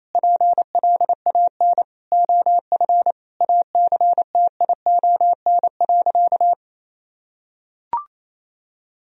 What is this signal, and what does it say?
Morse code, heard as a keyed tone, about 28 wpm, 700 Hz. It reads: PLAN OF ACTION.
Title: Sentences from Top 1000 Words - Review Part 4 - 28wpm